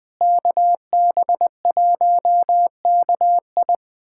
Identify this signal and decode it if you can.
Morse code, heard as a keyed tone, about 20 wpm, 700 Hz. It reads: KB1KI
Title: USA Callsign DE Callsign - 5 Characters - 20wpm